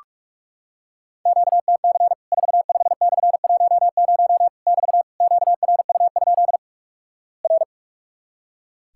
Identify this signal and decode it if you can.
Morse code, heard as a keyed tone, about 46 wpm, 700 Hz. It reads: QTC 45/10 <BT> QRU? R
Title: Worked All Europe DX Contest - QTC Exchange - 45wpm